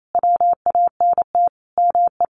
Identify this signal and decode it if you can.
Morse code, heard as a keyed tone, about 28 wpm, 700 Hz. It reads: WANT ME
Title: Sets of 5 Words Encore2 28wpm